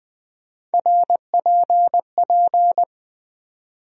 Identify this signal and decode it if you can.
Morse code, heard as a keyed tone, about 20 wpm, 700 Hz. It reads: RPP
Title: USA Call Sign Postfixes - 3 Letters 20wpm